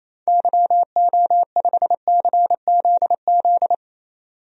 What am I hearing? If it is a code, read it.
Morse code, heard as a keyed tone, about 27 wpm, 700 Hz. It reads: YO5CZZ
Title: DXCC-100 Call Signs - 6 Characters - 28wpm